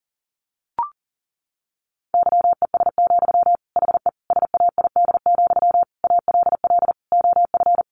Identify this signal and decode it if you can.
Morse code, heard as a keyed tone, about 40 wpm, 700 Hz. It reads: YES, HE SAID, ALL OF
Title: Sentences from Top 500 Words 40wpm